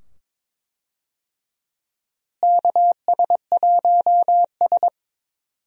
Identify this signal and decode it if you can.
Morse code, heard as a keyed tone, about 21 wpm, 700 Hz. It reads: KS1S
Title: USA Call Signs - 4 Characters - Repeated 2x - 22wpm